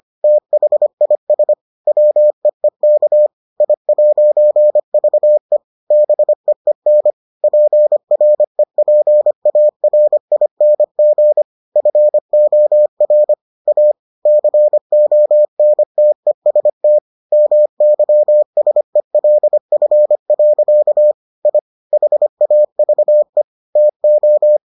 Morse code, heard as a keyed tone, about 25 wpm, 600 Hz. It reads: THIS WEEK I'VE BEEN PREPARING FOR A CONTEST MYSELF. I HAVE TO